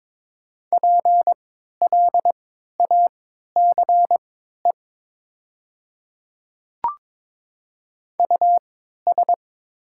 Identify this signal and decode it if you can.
Morse code, heard as a keyed tone, about 22 wpm, spaced as for 12 wpm, 700 Hz. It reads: PLACE US
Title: 3x Character Spacing - Top 200 Words - 22wpm